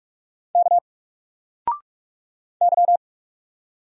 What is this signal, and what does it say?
Morse code, heard as a keyed tone, about 45 wpm, 700 Hz. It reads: K Y
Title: Single Character (Letter:Number:Punctuation:Prosign) 45wpm